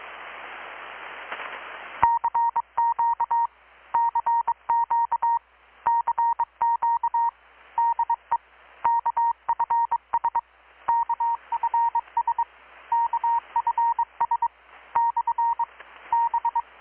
Morse code, heard as a keyed tone, about 21 wpm, 930 Hz. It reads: CQ CQ CQ DE KFS KFS KFS / B
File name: blog / swl / 6-abril-2019-12695-USB-KPS